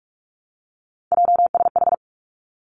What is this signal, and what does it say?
Morse code, heard as a keyed tone, about 46 wpm, 700 Hz. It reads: JSH
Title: Three Challenging Letter-Number Combinations 45wpm